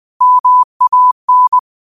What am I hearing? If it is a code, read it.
Morse code, heard as a keyed tone, about 20 wpm, 1000 Hz. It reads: MAN